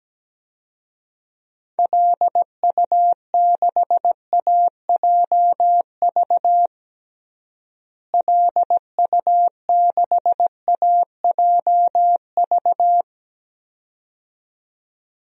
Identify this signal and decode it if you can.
Morse code, heard as a keyed tone, about 17 wpm, 700 Hz. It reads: LU6AJV LU6AJV
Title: DXCC-100 Call Signs - 6 Characters - Repeated 2x - 17wpm